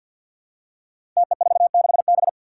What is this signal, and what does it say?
Morse code, heard as a keyed tone, about 51 wpm, 700 Hz. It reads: TE46B
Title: Five Challenging Letter-Number Combinations 50wpm